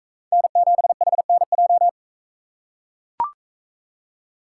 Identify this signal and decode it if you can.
Morse code, heard as a keyed tone, about 43 wpm, 700 Hz. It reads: N7HNJ